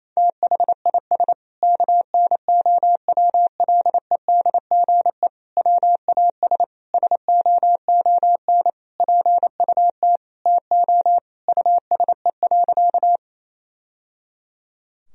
Morse code, heard as a keyed tone, about 28 wpm, 690 Hz. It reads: THIS KNOWLEDGE WAS SOON PUT TO USE.
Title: Sentences from Top 1000 Words - Review Part 3 - 28wpm